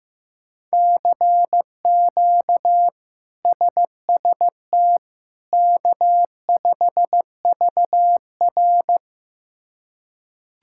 Morse code, heard as a keyed tone, about 15 wpm, 700 Hz. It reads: CQ SST K5VR
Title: K1USN SST QSO Practice - 15wpm